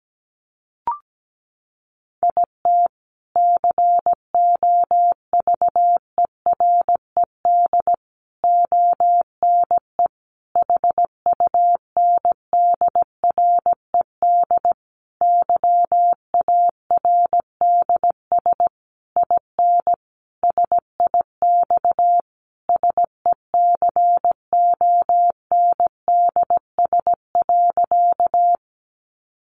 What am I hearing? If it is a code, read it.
Morse code, heard as a keyed tone, about 17 wpm, 700 Hz. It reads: IT COVERED ONE HUNDRED YARDS IN SIX SECONDS.
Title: A Sound Of Thunder 17wpm